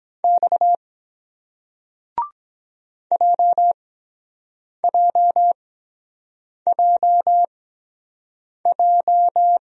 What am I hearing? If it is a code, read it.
Morse code, heard as a keyed tone, about 23 wpm, spaced as for 7 wpm, 700 Hz. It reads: X JJJJ